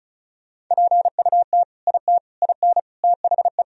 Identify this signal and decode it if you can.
Morse code, heard as a keyed tone, about 36 wpm, 700 Hz. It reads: PUT IT IN THE
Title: Sentences from Top 1000 Words - Review Part 2 - 35wpm